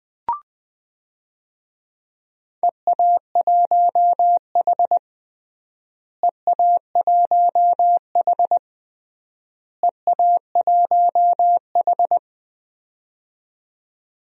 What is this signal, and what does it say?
Morse code, heard as a keyed tone, about 20 wpm, 700 Hz. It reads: EA1H EA1H EA1H
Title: DXCC-100 Call Signs - 4 Characters - Repeated 3x - 20wpm